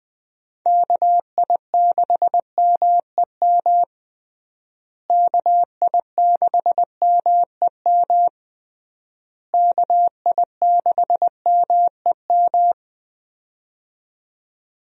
Morse code, heard as a keyed tone, about 20 wpm, 700 Hz. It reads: KI6MEM KI6MEM KI6MEM
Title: USA Call Signs - 6 Characters - Repeated 3x - 20wpm